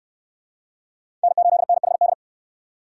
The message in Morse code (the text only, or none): N8NHD